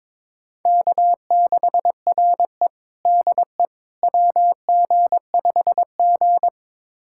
Morse code, heard as a keyed tone, about 22 wpm, 700 Hz. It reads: K6RE DE WG5G
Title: USA Callsign DE Callsign - 4 Characters - 22wpm